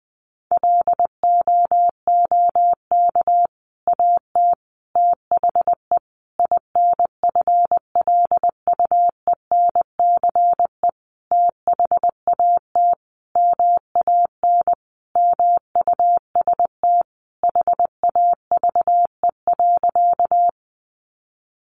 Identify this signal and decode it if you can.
Morse code, heard as a keyed tone, about 20 wpm, 700 Hz. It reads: LOOK AT THE INFLUENCE THAT MAN MUST HAVE.
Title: Sentences from Top 1100 Words 20wpm